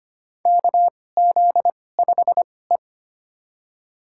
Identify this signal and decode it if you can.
Morse code, heard as a keyed tone, about 25 wpm, spaced as for 18 wpm, 700 Hz. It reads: KZ5E
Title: USA Call Signs - 4 Characters - 2x Character Spacing - 25wpm